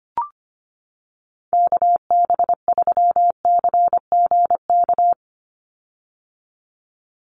Morse code, heard as a keyed tone, about 25 wpm, 700 Hz. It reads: KB3CGK
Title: USA Call Signs 25wpm